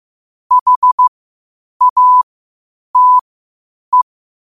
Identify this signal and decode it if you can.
Morse code, heard as a keyed tone, about 15 wpm, spaced as for 8 wpm, 1000 Hz. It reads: HATE
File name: qotdfiles / 8wpm